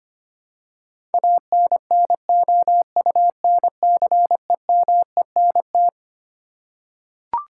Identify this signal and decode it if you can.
Morse code, heard as a keyed tone, about 25 wpm, 700 Hz. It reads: ANNOUNCEMENT
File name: Twelve Letter Words - 25wpm